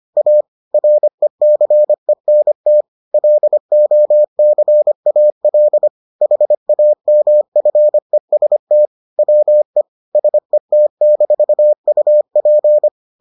Morse code, heard as a keyed tone, about 25 wpm, 600 Hz. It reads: A RECENT LOCAL HAMFEST WE SET-UP